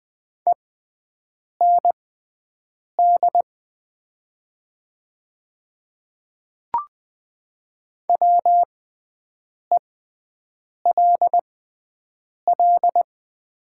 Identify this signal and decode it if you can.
Morse code, heard as a keyed tone, about 20 wpm, spaced as for 7 wpm, 700 Hz. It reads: END WELL
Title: 6x Character Spacing - Top 200 Words - 20wpm